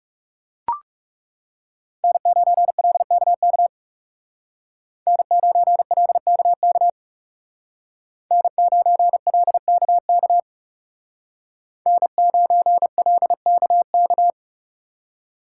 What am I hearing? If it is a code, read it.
Morse code, heard as a keyed tone, about 37 wpm, 700 Hz. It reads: N9LKK N9LKK N9LKK N9LKK
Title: Speed Racing - USA Call Signs 30wpm